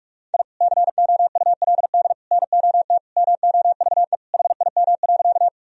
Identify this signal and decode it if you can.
Morse code, heard as a keyed tone, about 48 wpm, 700 Hz. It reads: I COULD NOT MOVE HIM.
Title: Sentences from Top 1000 Words - Review Part 1 - 45wpm